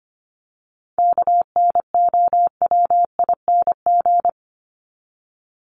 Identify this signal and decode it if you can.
Morse code, heard as a keyed tone, about 25 wpm, 700 Hz. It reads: KNOWING
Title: ING Suffix - Top 100 Words with Base Words 25wpm